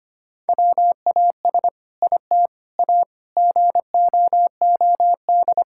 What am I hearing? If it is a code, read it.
Morse code, heard as a keyed tone, about 25 wpm, 700 Hz. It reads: WAS IT A GOOD
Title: Sentences from Top 1000 Words - Review Part 5 - 25wpm